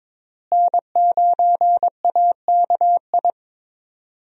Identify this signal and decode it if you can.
Morse code, heard as a keyed tone, about 23 wpm, 700 Hz. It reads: N9AKI
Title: USA Call Signs - 5 Characters - Repeated 3x - 22wpm